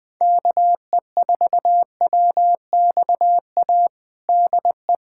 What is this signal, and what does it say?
Morse code, heard as a keyed tone, about 20 wpm, 700 Hz. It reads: KE4WXA DE
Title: USA Callsign DE Callsign - 6 Characters - 20wpm